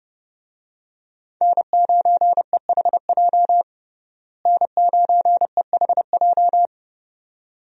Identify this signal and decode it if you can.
Morse code, heard as a keyed tone, about 30 wpm, 700 Hz. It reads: N9EHJ N9EHJ
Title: USA Call Signs - 5 Characters - Repeated 2x - 30wpm